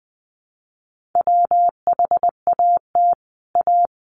Morse code, heard as a keyed tone, about 20 wpm, 700 Hz. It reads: WHAT A
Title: Sentences from Top 1500 Words 20wpm